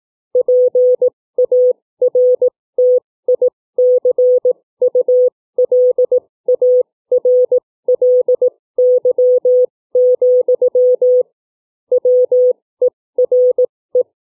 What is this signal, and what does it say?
Morse code, heard as a keyed tone, about 18 wpm, spaced as for 15 wpm, 500 Hz. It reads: PARTICULARLY, WERE
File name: CW / book reading